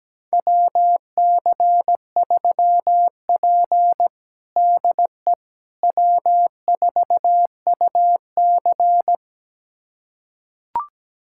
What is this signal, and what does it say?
Morse code, heard as a keyed tone, about 17 wpm, 700 Hz. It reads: WC3P DE W4UC
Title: USA Callsign DE Callsign - 4 Characters - 17wpm